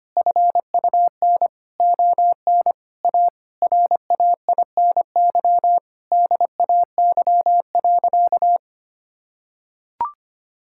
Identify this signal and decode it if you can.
Morse code, heard as a keyed tone, about 25 wpm, 700 Hz. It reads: FUN ON A RAINY DAY.
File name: practice / 25wpm / Difficult Sentences with Hints - 25wpm